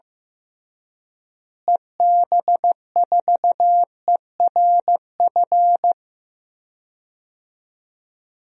Morse code, heard as a keyed tone, about 15 wpm, 700 Hz. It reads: EB4ERF
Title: DXCC-100 Call Signs - 6 Characters - Repeated 2x - 15wpm